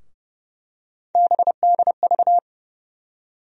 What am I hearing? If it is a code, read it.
Morse code, heard as a keyed tone, about 30 wpm, 700 Hz. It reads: BDV
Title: USA Call Sign Postfixes - 3 Letters 30wpm